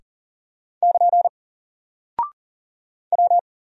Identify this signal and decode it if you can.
Morse code, heard as a keyed tone, about 40 wpm, 700 Hz. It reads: <KN> W
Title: Single Character (Letter:Number:Punctuation:Prosign) 40wpm